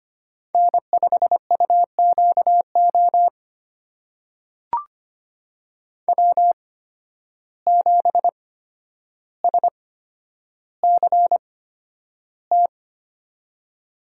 Morse code, heard as a keyed tone, about 25 wpm, spaced as for 11 wpm, 700 Hz. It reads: N5UQO W7SCT
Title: USA Call Signs - 8x Character Spacing - 25wpm